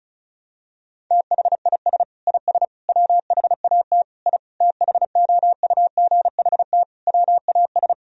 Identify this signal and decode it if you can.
Morse code, heard as a keyed tone, about 34 wpm, 700 Hz. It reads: THIS IS WHAT I THOUGHT WAS
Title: Sentences from Top 1000 Words - Review Part 4 - 35wpm